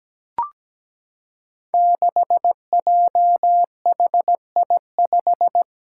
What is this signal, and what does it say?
Morse code, heard as a keyed tone, about 17 wpm, 700 Hz. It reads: 6JHI5
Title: Five Challenging Letter-Number Combinations 17wpm